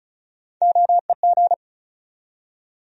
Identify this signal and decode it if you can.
Morse code, heard as a keyed tone, about 34 wpm, 700 Hz. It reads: OEG
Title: USA Call Sign Postfixes - 3 Letters 35wpm